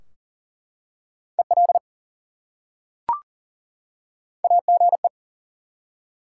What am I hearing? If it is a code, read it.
Morse code, heard as a keyed tone, about 40 wpm, 700 Hz. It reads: EL AGE
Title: Top 180 QSO Elements 40wpm